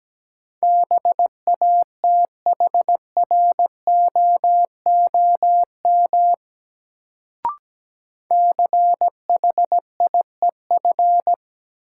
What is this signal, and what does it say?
Morse code, heard as a keyed tone, about 17 wpm, 700 Hz. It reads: BATHROOM CHIEF
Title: Sets of 5 Words Encore - New Word Drill - 17wpm